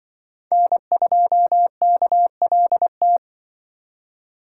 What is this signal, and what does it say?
Morse code, heard as a keyed tone, about 24 wpm, 700 Hz. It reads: N2KLT
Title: Speed Racing - USA Call Signs 20wpm